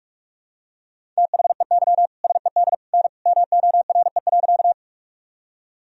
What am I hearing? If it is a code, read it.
Morse code, heard as a keyed tone, about 44 wpm, 700 Hz. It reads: THEY SED N MORE.
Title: Sentences from Top 100 Words and Abbreviations 45wpm